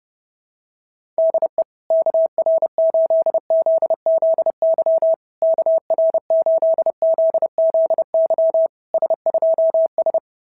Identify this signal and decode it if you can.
Morse code, heard as a keyed tone, about 30 wpm, 690 Hz. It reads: DE KR8ZZY KR8ZZY S2S
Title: SOTA QSO Practice 30wpm